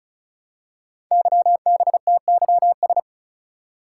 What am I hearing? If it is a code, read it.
Morse code, heard as a keyed tone, about 36 wpm, 700 Hz. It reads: YBTYS
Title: Code Groups - 5 Letters - 35wpm